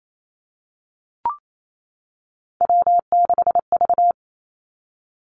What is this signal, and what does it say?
Morse code, heard as a keyed tone, about 28 wpm, 700 Hz. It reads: W6V
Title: Three Challenging Letter-Number Combinations 28wpm